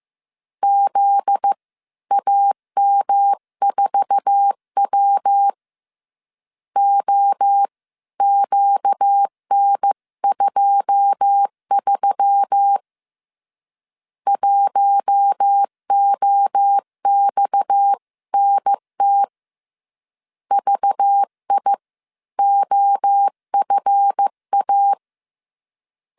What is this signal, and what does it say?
Morse code, heard as a keyed tone, about 15 wpm, spaced as for 13 wpm, 800 Hz. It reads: ZAM4W OQN23 1OXNT VIOFA
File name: CW / DET / 1 9 1500C S C 14wpm 3245